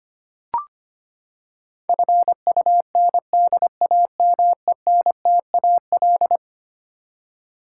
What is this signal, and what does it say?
Morse code, heard as a keyed tone, about 25 wpm, 700 Hz. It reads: FUNDAMENTAL
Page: morse-code-ninja-practice-25wpm